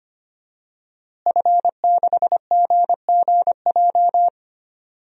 Morse code, heard as a keyed tone, about 25 wpm, 700 Hz. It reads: F6GGJ
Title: DXCC-100 Call Signs - 5 Characters - Repeated 2x - 25wpm